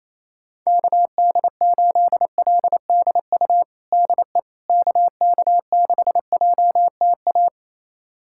Morse code, heard as a keyed tone, about 28 wpm, 700 Hz. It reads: KD8LDU DE KK6JTA
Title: USA Callsign DE Callsign - 6 Characters - 28wpm